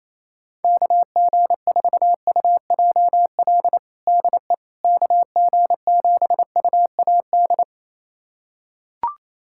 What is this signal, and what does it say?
Morse code, heard as a keyed tone, about 28 wpm, 700 Hz. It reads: KG4UJL DE KG7UAD
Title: USA Callsign DE Callsign 28wpm